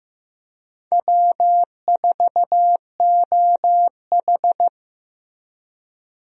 Morse code, heard as a keyed tone, about 16 wpm, 700 Hz. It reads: W4OH
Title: USA Call Signs - 4 Characters - Repeated 3x - 15wpm